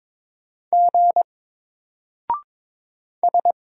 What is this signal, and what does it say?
Morse code, heard as a keyed tone, about 22 wpm, 700 Hz. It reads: G S
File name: Single-Letter-Number-22wpm